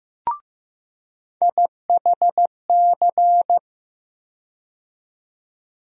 Morse code, heard as a keyed tone, about 15 wpm, 700 Hz. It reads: IHC
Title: USA Call Sign Postfixes - 3 Letters 15wpm